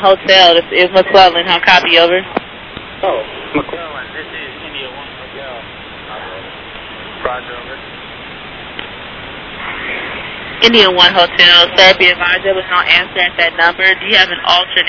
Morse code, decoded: none

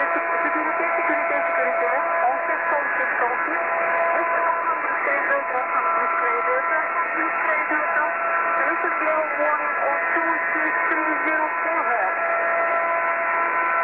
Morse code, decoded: none

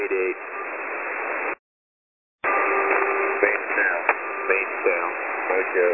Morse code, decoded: none

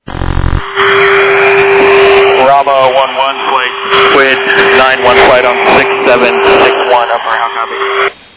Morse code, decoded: none

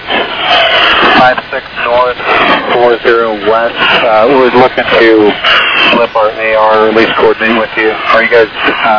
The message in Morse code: none

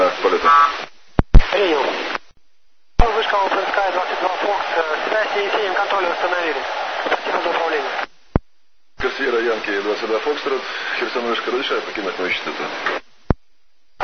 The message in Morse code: none

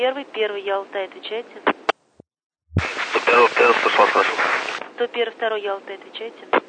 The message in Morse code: none

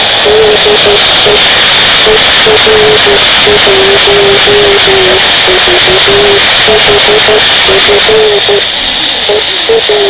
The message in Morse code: none

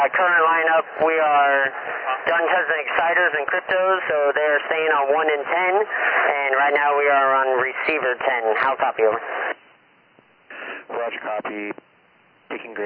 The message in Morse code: none

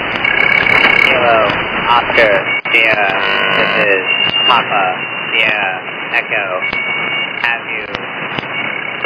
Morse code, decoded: none

